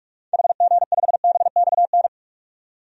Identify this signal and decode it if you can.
Morse code, heard as a keyed tone, about 42 wpm, 700 Hz. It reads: HG5BXN
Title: DXCC-100 Call Signs - 6 Characters - 45wpm